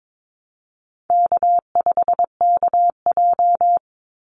Morse code, heard as a keyed tone, about 22 wpm, 700 Hz. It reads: K5KJ
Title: Easy Call Signs Repeated 2x - 22wpm